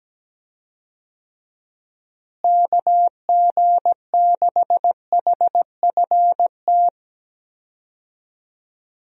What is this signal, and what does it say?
Morse code, heard as a keyed tone, about 17 wpm, 700 Hz. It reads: KG6HFT
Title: USA Call Signs - 6 Characters - Repeated 3x - 17wpm